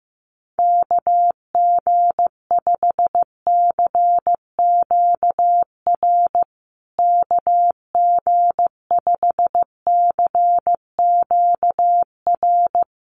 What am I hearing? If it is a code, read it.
Morse code, heard as a keyed tone, about 15 wpm, 700 Hz. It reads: KG5CQR KG5CQR
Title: USA Call Signs Repeated 3x 15wpm